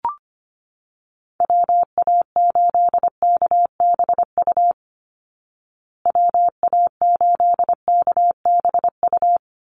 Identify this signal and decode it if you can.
Morse code, heard as a keyed tone, about 25 wpm, 700 Hz. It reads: WA8KBU WA8KBU
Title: Call Signs Repeated 3x - 25wpm